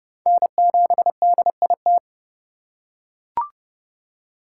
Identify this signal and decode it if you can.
Morse code, heard as a keyed tone, about 30 wpm, 700 Hz. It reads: N7DIT
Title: USA Call Signs - 5 Characters - 3x Character Spacing - 30wpm